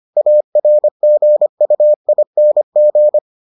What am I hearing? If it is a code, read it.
Morse code, heard as a keyed tone, about 25 wpm, 600 Hz. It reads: ARGUING